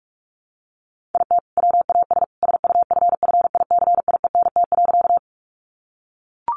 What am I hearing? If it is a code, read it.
Morse code, heard as a keyed tone, about 44 wpm, 700 Hz. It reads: IT WAS SUFFICIENT.
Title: Sentences from Top 1400 Words 45wpm